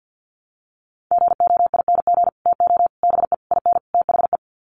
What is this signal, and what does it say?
Morse code, heard as a keyed tone, about 50 wpm, 700 Hz. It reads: GOING TO BE IN THE
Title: Sets of 5 Words Encore 50wpm